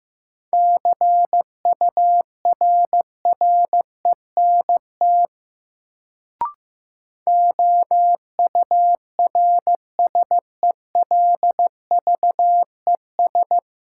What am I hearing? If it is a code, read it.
Morse code, heard as a keyed tone, about 15 wpm, 700 Hz. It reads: CURRENT OURSELVES
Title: Sentences from Top 1600 Words - New Word Drill Alt - 15wpm